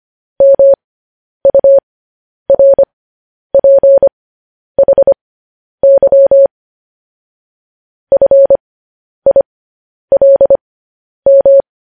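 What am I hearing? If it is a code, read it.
Morse code, heard as a keyed tone, about 25 wpm, spaced as for 10 wpm, 550 Hz. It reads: MURPHY FILM
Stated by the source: Morse code news headlines transmission.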